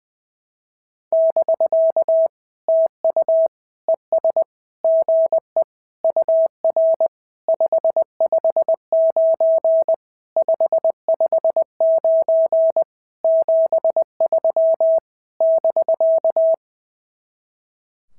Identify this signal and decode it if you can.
Morse code, heard as a keyed tone, about 20 wpm, 650 Hz. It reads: <BK> TU ES GE UR 559 559 73 <BK>
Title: SOTA QSO Practice 20wpm